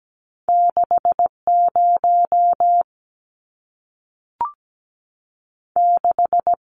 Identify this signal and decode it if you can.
Morse code, heard as a keyed tone, about 17 wpm, 700 Hz. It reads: 6 0 6
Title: Numbers - 2-Digit 17wpm